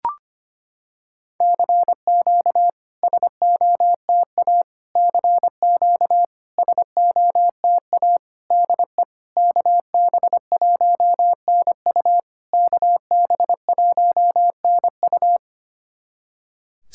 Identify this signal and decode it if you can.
Morse code, heard as a keyed tone, about 25 wpm, 690 Hz. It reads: CQ SOTA CQ SOTA DE KB1NU KB1NU
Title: SOTA QSO Practice 25wpm